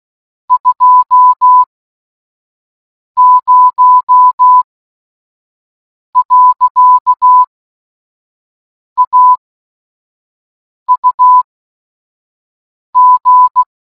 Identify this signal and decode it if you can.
Morse code, heard as a keyed tone, about 16 wpm, spaced as for 5 wpm, 1000 Hz. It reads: 20.AUG